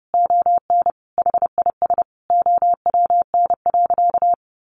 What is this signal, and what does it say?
Morse code, heard as a keyed tone, about 30 wpm, 700 Hz. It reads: ON HIS OWN.
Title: Sentences from Top 1400 Words 30wpm